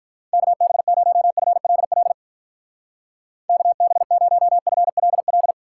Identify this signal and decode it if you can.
Morse code, heard as a keyed tone, about 43 wpm, 700 Hz. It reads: KB0FLL KB0FLL
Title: Speed Racing - USA 2x3 Call Signs 35wpm